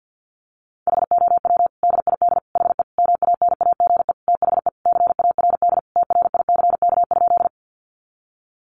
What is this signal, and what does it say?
Morse code, heard as a keyed tone, about 50 wpm, 700 Hz. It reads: HOW DID HE MANAGE THE CARD TRICK?
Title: Sentences from Top 1500 Words 50wpm